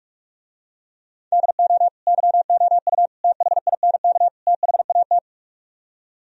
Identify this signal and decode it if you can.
Morse code, heard as a keyed tone, about 48 wpm, 700 Hz. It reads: DO YOU THINK THAT